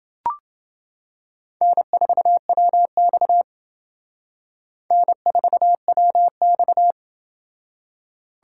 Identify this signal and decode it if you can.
Morse code, heard as a keyed tone, about 28 wpm, 700 Hz. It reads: N4WX N4WX